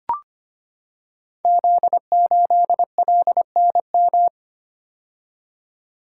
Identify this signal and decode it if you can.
Morse code, heard as a keyed tone, about 25 wpm, 700 Hz. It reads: Z8LNM
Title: Code Groups - 5 Letter:Numbers - 25wpm